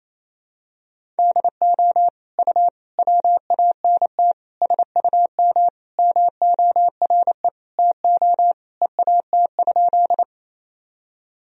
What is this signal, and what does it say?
Morse code, heard as a keyed tone, about 27 wpm, 700 Hz. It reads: DO U WANT SUM MORE TO EAT?
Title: Sentences from Top 300 Words and Abbreviations 28wpm